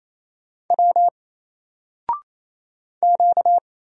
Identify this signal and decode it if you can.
Morse code, heard as a keyed tone, about 27 wpm, 700 Hz. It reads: W Q
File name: Single-Letter-Number-28wpm